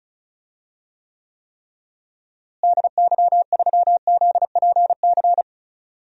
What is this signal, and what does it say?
Morse code, heard as a keyed tone, about 36 wpm, 700 Hz. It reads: DY3ZPC
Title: DXCC-100 Call Signs - 6 Characters - Repeated 3x - 35wpm